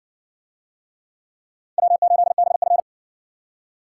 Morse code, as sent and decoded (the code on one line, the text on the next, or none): .-- ---.. -... ..-.
W8BF